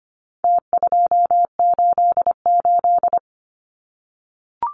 Morse code, heard as a keyed tone, about 25 wpm, 700 Hz. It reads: T288
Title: Numbers - Contest Style - 4-Digit 25wpm